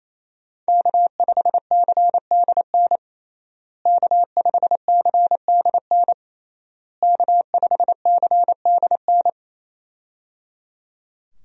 Morse code, heard as a keyed tone, about 27 wpm, 700 Hz. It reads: K5CDN K5CDN K5CDN
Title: USA Call Signs - 5 Characters - Repeated 3x - 28wpm